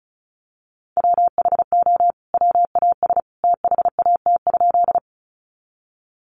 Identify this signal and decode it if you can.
Morse code, heard as a keyed tone, about 34 wpm, 700 Hz. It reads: WHO WAS THAT?
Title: Sentences from Top 100 Words 35wpm